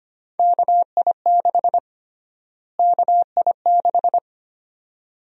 Morse code, heard as a keyed tone, about 25 wpm, 700 Hz. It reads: KI6 KI6
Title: USA Call Sign Prefixes - Repeated 2x - 25wpm